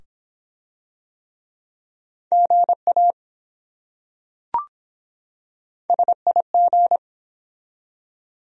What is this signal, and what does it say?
Morse code, heard as a keyed tone, about 23 wpm, 700 Hz. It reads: GA SIG